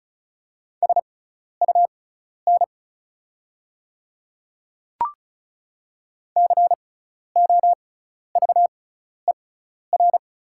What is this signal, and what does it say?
Morse code, heard as a keyed tone, about 36 wpm, spaced as for 12 wpm, 690 Hz. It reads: SUN COVER
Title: 6x Character Spacing - Top 300 Words - 35wpm